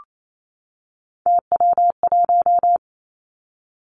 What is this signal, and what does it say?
Morse code, heard as a keyed tone, about 28 wpm, 700 Hz. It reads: TW1
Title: Three Challenging Letter-Number Combinations 28wpm